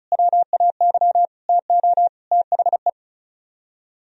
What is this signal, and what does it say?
Morse code, heard as a keyed tone, about 44 wpm, 700 Hz. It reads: WAY TO THE
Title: Speed-Racing - Sets of 4 Words 35wpm